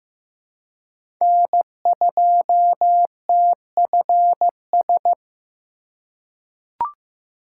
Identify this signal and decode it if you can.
Morse code, heard as a keyed tone, about 15 wpm, 700 Hz. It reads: N2TFS
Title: USA Call Signs - 5 Characters - 2x Character Spacing - 15wpm